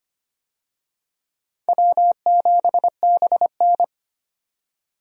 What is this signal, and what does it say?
Morse code, heard as a keyed tone, about 25 wpm, 700 Hz. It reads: W7BN
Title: USA Call Signs - 4 Characters - 25wpm